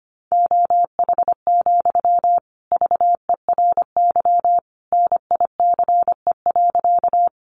Morse code, heard as a keyed tone, about 25 wpm, 700 Hz. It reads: OH, VERY NICE.
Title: Sentences from Top 1100 Words 25wpm